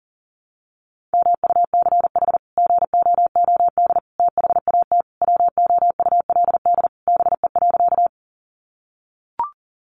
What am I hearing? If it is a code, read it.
Morse code, heard as a keyed tone, about 40 wpm, 700 Hz. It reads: MUCH GOOD THAT WOULD BE.
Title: Sentences from Top 200 Words 40wpm